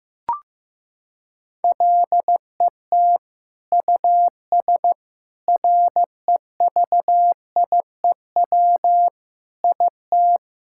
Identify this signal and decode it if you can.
Morse code, heard as a keyed tone, about 15 wpm, 700 Hz. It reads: LET US REVIEW IT
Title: Sentences from Top 1000 Words - Review Part 6 - 15wpm